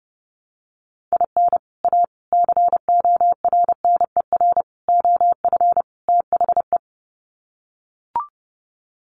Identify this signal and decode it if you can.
Morse code, heard as a keyed tone, about 30 wpm, 700 Hz. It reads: IN A CORNER OF THE
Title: Sets of 5 Words Encore2 30wpm